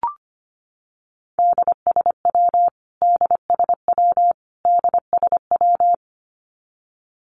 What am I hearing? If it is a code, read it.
Morse code, heard as a keyed tone, about 25 wpm, 700 Hz. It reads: DSW DSW DSW
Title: Abbreviations Repeated 3x - 25wpm